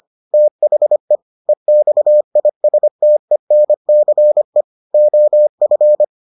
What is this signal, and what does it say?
Morse code, heard as a keyed tone, about 25 wpm, 600 Hz. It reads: THE EXISTENCE OF